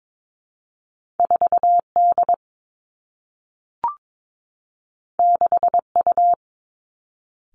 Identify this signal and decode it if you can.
Morse code, heard as a keyed tone, about 22 wpm, 700 Hz. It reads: 4 D 6 U
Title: Two Challenging Letter-Number Combinations 22wpm